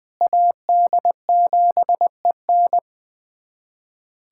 Morse code, heard as a keyed tone, about 20 wpm, 700 Hz. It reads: AD7EN
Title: USA Callsign DE Callsign - 5 Characters - 20wpm